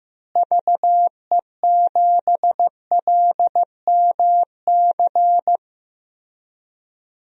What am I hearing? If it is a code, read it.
Morse code, heard as a keyed tone, about 15 wpm, 700 Hz. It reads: VE7LMC
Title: DXCC-100 Call Signs - 6 Characters - Repeated 3x - 15wpm